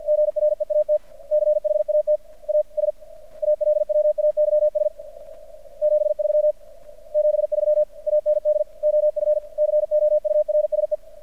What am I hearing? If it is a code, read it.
Morse code, heard as a keyed tone, about 50 wpm, 600 Hz. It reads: GREAT CHAT AS ALWAYS 73 73 AND OF COURSE
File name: archived / CRSnet